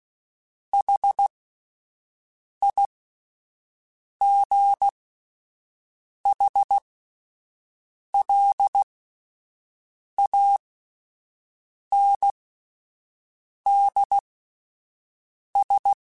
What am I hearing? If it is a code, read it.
Morse code, heard as a keyed tone, about 16 wpm, spaced as for 5 wpm, 800 Hz. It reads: HIGHLANDS